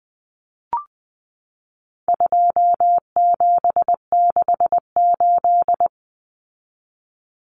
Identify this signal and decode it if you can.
Morse code, heard as a keyed tone, about 20 wpm, 700 Hz. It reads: 2768
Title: Common Cut Numbers - 4-Digit 20wpm